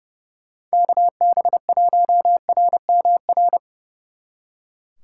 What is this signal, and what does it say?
Morse code, heard as a keyed tone, about 36 wpm, 700 Hz. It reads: KB1RMR
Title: Speed Racing - USA 2x3 Call Signs 30wpm